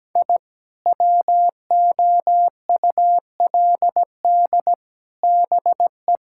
Morse code, heard as a keyed tone, about 17 wpm, 700 Hz. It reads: I WOULD BE
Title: Sentences from Top 1000 Words - Review Part 5 - 17wpm